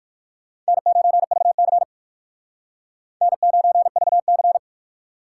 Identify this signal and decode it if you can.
Morse code, heard as a keyed tone, about 56 wpm, 700 Hz. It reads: N9VC N9VC